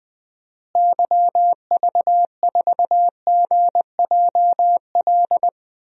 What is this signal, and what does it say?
Morse code, heard as a keyed tone, about 20 wpm, 700 Hz. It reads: YV4GJL
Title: DXCC-100 Call Signs - 6 Characters - Repeated 2x - 20wpm